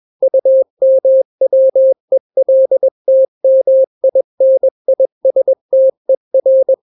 Morse code, heard as a keyed tone, about 21 wpm, 530 Hz. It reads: UMWELTMINISTER
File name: daily / 20wpm